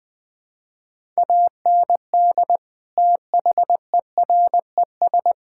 Morse code, heard as a keyed tone, about 20 wpm, 700 Hz. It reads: AND THERES
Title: Sets of 8 Words - 20wpm